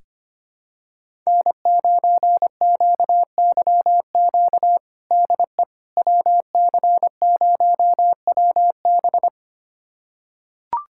Morse code, heard as a keyed tone, about 25 wpm, 700 Hz. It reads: N9QYQ DE WC0WB
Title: USA Callsign DE Callsign - 5 Characters - 25wpm